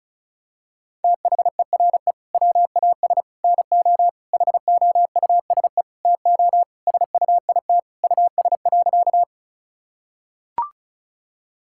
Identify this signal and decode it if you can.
Morse code, heard as a keyed tone, about 36 wpm, 700 Hz. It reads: THERE WAS NO HOUSE TO SUIT US.
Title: Sentences from Top 1600 Words 35wpm